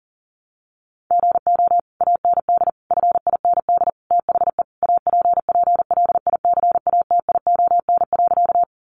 Morse code, heard as a keyed tone, about 40 wpm, 700 Hz. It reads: GO AND FIND THE APPLICATION.
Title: Sentences from Top 1200 Words 40wpm